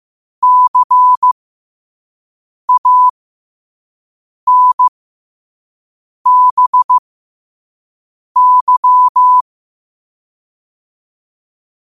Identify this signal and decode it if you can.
Morse code, heard as a keyed tone, about 15 wpm, spaced as for 5 wpm, 1000 Hz. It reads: CANBY
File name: qotdfiles / 5wpm